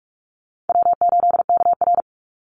Morse code, heard as a keyed tone, about 44 wpm, 700 Hz. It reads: W8KR
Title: Easy Call Signs Repeated 3x - 45wpm